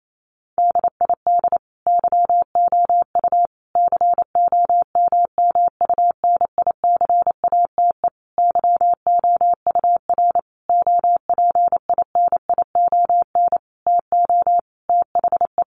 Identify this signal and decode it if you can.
Morse code, heard as a keyed tone, about 28 wpm, 700 Hz. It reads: DID YOU COMMUNICATE YOUR OPINION TO THE
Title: Sentences from Top 1300 Words 28wpm